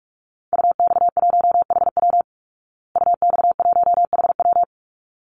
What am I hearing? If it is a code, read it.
Morse code, heard as a keyed tone, about 44 wpm, 700 Hz. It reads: UX1HW UX1HW
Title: Call Signs Repeated 3x - 45wpm